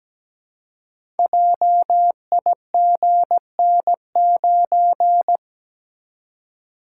Morse code, heard as a keyed tone, about 17 wpm, 700 Hz. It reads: JIGN9
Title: Code Groups - 5 Letter:Numbers - 17wpm